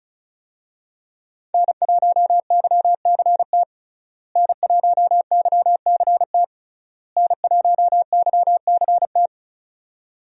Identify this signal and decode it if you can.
Morse code, heard as a keyed tone, about 36 wpm, 700 Hz. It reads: N1YCT N1YCT N1YCT
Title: USA Call Signs - 5 Characters - Repeated 3x - 35wpm